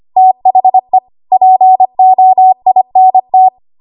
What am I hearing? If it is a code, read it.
Morse code, heard as a keyed tone, about 25 wpm, 750 Hz. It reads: THE POINT
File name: Archive / 25 WPM / 25WPM